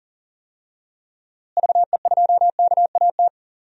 Morse code, heard as a keyed tone, about 38 wpm, 700 Hz. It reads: VE2KAT
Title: DXCC-100 Call Signs - 6 Characters - Repeated 2x - 40wpm